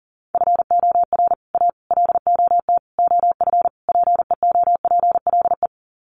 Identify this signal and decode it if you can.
Morse code, heard as a keyed tone, about 40 wpm, 700 Hz. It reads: FOR A LOT OF PEOPLE
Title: Sets of 5 Words Encore 40wpm